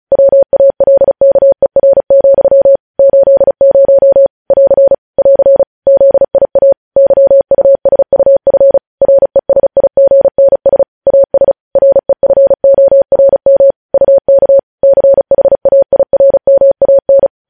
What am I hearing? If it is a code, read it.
Morse code, heard as a keyed tone, about 36 wpm, 550 Hz. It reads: WALKER, 80 <AR> <AR> ZIA YUSUF RESIGNS AS REFORM UK CHAIRMAN
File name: podcast / News headlines-1x-35x35